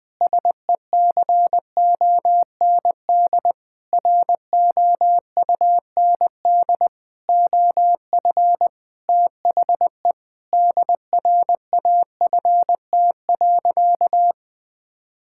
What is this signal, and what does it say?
Morse code, heard as a keyed tone, about 20 wpm, 700 Hz. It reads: SECOND ROUND OF THE DRAFT.
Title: Sentences with 6-Word Phrases - Spoken Beginning - 20wpm